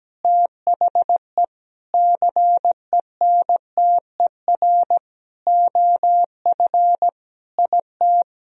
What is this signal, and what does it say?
Morse code, heard as a keyed tone, about 17 wpm, 700 Hz. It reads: THE CENTER OF IT
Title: Sentences from Top 1000 Words - Review Part 2 - 17wpm